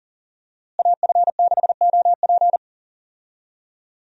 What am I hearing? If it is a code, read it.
Morse code, heard as a keyed tone, about 37 wpm, 700 Hz. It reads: AF6OP
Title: USA Call Signs - 5 Characters - Repeated 2x - 40wpm